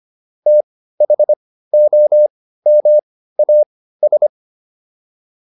Morse code, heard as a keyed tone, about 25 wpm, spaced as for 15 wpm, 600 Hz. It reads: THOMAS